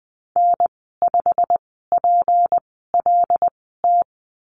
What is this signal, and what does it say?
Morse code, heard as a keyed tone, about 20 wpm, spaced as for 14 wpm, 700 Hz. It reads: N5PLT
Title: 161 Lesson 26g - Letter g - Call Signs 20wpm